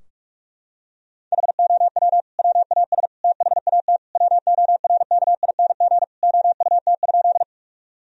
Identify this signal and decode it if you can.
Morse code, heard as a keyed tone, about 46 wpm, 700 Hz. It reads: HOW WAS THAT WORKING OUT?
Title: Sentences from Top 1000 Words - Review Part 4 - 45wpm